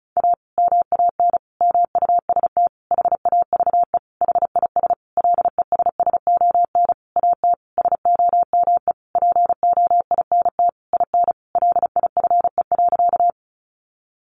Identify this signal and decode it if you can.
Morse code, heard as a keyed tone, about 34 wpm, 700 Hz. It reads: A MAN MUST HAVE HIS LESSON AT SOME POINT IN LIFE.
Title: Sentences from Top 1200 Words 35wpm